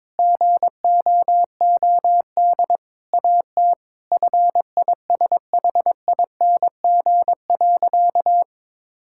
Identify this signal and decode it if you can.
Morse code, heard as a keyed tone, about 22 wpm, 700 Hz. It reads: GOOD AT FISHING.
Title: Sentences from Top 1000 Words - Review Part 5 - 22wpm